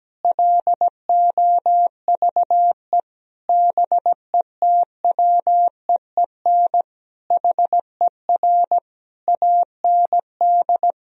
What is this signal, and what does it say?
Morse code, heard as a keyed tone, about 17 wpm, 700 Hz. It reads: LOVE BETWEEN HER AND
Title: Sentences from Top 1000 Words - Review Part 1 - 17wpm